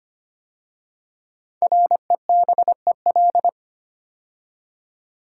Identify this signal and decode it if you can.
Morse code, heard as a keyed tone, about 25 wpm, 700 Hz. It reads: REBEL
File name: 5 Letter Words - First 3 Letters Given - 25wpm